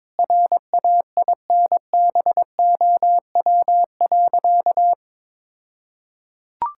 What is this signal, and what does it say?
Morse code, heard as a keyed tone, about 22 wpm, 700 Hz. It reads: RAINBOW.
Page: morse-code-ninja-practice-22wpm